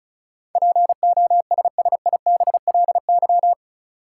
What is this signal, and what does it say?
Morse code, heard as a keyed tone, about 36 wpm, 700 Hz. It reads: POSSIBLY
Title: Sentences from Top 1300 Words - New Word Drill - 35wpm